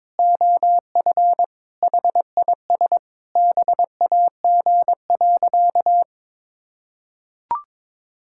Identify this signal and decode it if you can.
Morse code, heard as a keyed tone, about 22 wpm, 700 Hz. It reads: OF HIS BAG.
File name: practice / 22wpm / Sentences with 3-Word Phrases - 22wpm